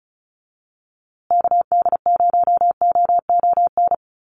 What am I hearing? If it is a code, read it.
Morse code, heard as a keyed tone, about 34 wpm, 700 Hz. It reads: KD0OON
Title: USA Call Signs Repeated 2x 35wpm